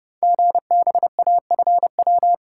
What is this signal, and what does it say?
Morse code, heard as a keyed tone, about 30 wpm, 700 Hz. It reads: GBAFW
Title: Code Groups - 5 Letters - 30wpm